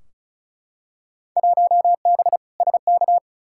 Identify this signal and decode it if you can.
Morse code, heard as a keyed tone, about 36 wpm, 700 Hz. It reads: 1B SK
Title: Field Day Exchanges Encore - 35wpm